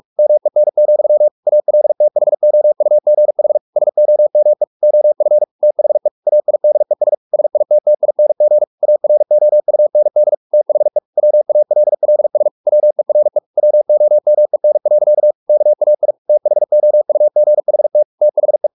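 Morse code, heard as a keyed tone, about 44 wpm, 600 Hz. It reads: MEN, ALTHOUGH SOME OF THE AIDES SITTING AROUND THE WALLS WERE WOMEN. KAI THOUGHT THE
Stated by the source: Morse practice at 45 wpm, the novel Never by Ken Follett in English, from CW med Gnister.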